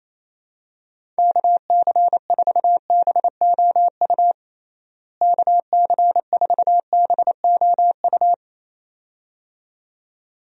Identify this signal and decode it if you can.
Morse code, heard as a keyed tone, about 27 wpm, 700 Hz. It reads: KC4BOU KC4BOU
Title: USA Call Signs - 6 Characters - Repeated 2x - 28wpm